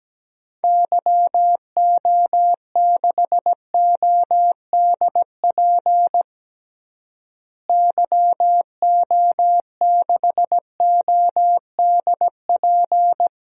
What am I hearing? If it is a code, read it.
Morse code, heard as a keyed tone, about 17 wpm, 700 Hz. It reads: YO6ODP YO6ODP
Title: DXCC-100 Call Signs - 6 Characters - Repeated 2x - 17wpm